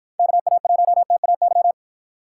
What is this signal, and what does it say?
Morse code, heard as a keyed tone, about 53 wpm, 700 Hz. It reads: KA1TAY